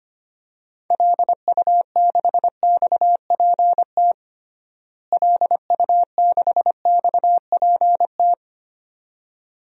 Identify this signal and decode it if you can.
Morse code, heard as a keyed tone, about 25 wpm, 700 Hz. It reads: LU6XPT LU6XPT
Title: DXCC-100 Call Signs - 6 Characters - Repeated 2x - 25wpm